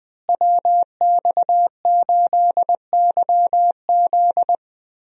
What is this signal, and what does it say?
Morse code, heard as a keyed tone, about 20 wpm, 700 Hz. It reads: WX8YZ
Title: USA Call Signs - 5 Characters - Repeated 2x - 20wpm